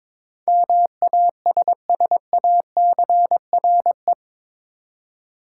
Morse code, heard as a keyed tone, about 22 wpm, 700 Hz. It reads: MASSACRE
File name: practice / 22wpm / Eight Letter Words - 22wpm